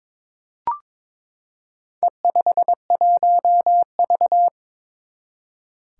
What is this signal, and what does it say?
Morse code, heard as a keyed tone, about 22 wpm, 700 Hz. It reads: E51V
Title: Four Challenging Letter-Number Combinations 22wpm